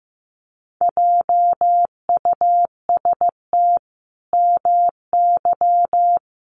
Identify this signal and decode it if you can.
Morse code, heard as a keyed tone, about 15 wpm, 700 Hz. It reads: JUST MY
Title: Sentences from Top 1200 Words 15wpm